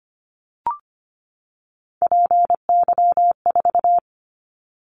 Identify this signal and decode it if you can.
Morse code, heard as a keyed tone, about 25 wpm, 700 Hz. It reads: PY4
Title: Callsign Prefixes for Top 100 DXCC - 25wpm